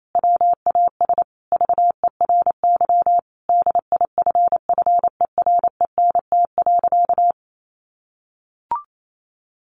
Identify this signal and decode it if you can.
Morse code, heard as a keyed tone, about 28 wpm, 700 Hz. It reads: WAS VERY DIFFERENT.
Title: Sentences from Top 1300 Words 28wpm